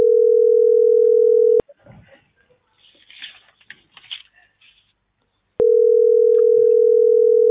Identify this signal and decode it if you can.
Ringback tone (North America), 440 + 480 Hz.